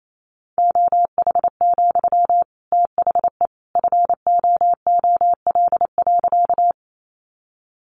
Morse code, heard as a keyed tone, about 28 wpm, 700 Hz. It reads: OH, THE FOOL.
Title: A Sound Of Thunder 28wpm